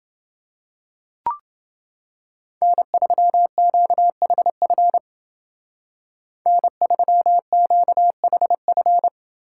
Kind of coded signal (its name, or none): Morse code